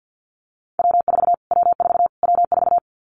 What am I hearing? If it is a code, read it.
Morse code, heard as a keyed tone, about 50 wpm, 700 Hz. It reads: W4 W4 W4
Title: Callsign Prefixes for Top 25 DXCC - Repeated 3x - 50wpm